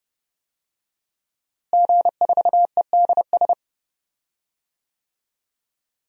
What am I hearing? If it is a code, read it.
Morse code, heard as a keyed tone, about 28 wpm, 700 Hz. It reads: G4EDS
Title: DXCC-100 Call Signs - 5 Characters - Repeated 2x - 30wpm